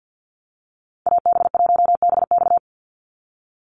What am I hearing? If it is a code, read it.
Morse code, heard as a keyed tone, about 50 wpm, 700 Hz. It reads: AB1BX
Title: Call Signs Repeated 3x - 50wpm